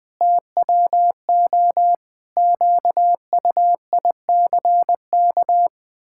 Morse code, heard as a keyed tone, about 21 wpm, 700 Hz. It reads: TWO QUICK
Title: Pangram Challenge v2 - 20wpm